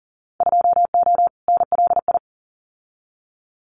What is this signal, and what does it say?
Morse code, heard as a keyed tone, about 40 wpm, 700 Hz. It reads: 2O NLI
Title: Winter Field Day Exchanges 40wpm